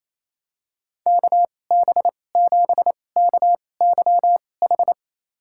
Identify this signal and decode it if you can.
Morse code, heard as a keyed tone, about 28 wpm, spaced as for 20 wpm, 700 Hz. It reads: KB7KYH
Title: USA Call Signs - 2x Character Spacing - 28wpm